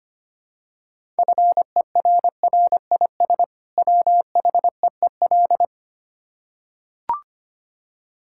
Morse code, heard as a keyed tone, about 25 wpm, 690 Hz. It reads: FERRIS WHEEL
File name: practice / 25wpm / Easy Words with Hints - 25wpm